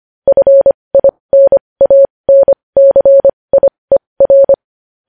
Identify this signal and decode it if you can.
Morse code, heard as a keyed tone, about 25 wpm, spaced as for 20 wpm, 550 Hz. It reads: FINANCIER
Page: News Headlines in Morse Code at 20 WPM